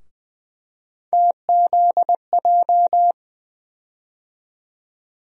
Morse code, heard as a keyed tone, about 20 wpm, 700 Hz. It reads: TZJ
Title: USA Call Sign Postfixes - 3 Letters 20wpm